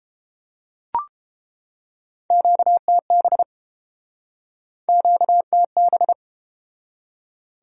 Morse code, heard as a keyed tone, about 31 wpm, 700 Hz. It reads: QTB QTB